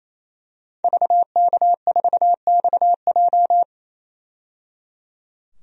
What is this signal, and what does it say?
Morse code, heard as a keyed tone, about 28 wpm, 700 Hz. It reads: VK4XJ
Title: DXCC-100 Call Signs - 5 Characters - 28wpm